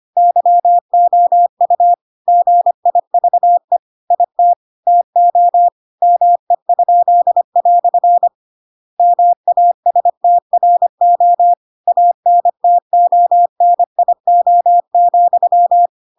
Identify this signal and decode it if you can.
Morse code, heard as a keyed tone, about 25 wpm, 700 Hz. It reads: YOU GIVE IT TO ME?" MASTRO ANTONIO,
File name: Pinocchio / Pinocchio ch2 25 25 ik1hge